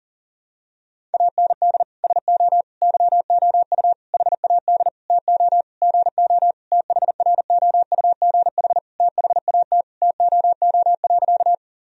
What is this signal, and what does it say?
Morse code, heard as a keyed tone, about 40 wpm, 700 Hz. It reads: AND SO YOU HAD TO GO THROUGH THAT TOO.
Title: Sentences from Top 1000 Words - Review Part 1 - 40wpm